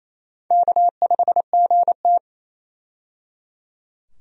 Morse code, heard as a keyed tone, about 28 wpm, 700 Hz. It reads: K5GT
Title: USA Call Signs - 4 Characters - 28wpm